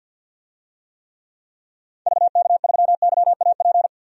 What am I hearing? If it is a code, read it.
Morse code, heard as a keyed tone, about 50 wpm, 700 Hz. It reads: VK3YAP
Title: DXCC-100 Call Signs - 6 Characters - 50wpm